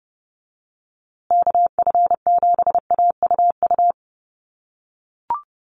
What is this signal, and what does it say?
Morse code, heard as a keyed tone, about 30 wpm, 700 Hz. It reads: KF7AUU
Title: USA Call Signs Repeated 3x 30wpm